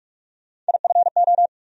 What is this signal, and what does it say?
Morse code, heard as a keyed tone, about 41 wpm, 700 Hz. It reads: IFO
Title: USA Call Sign Postfixes - 3 Letters 45wpm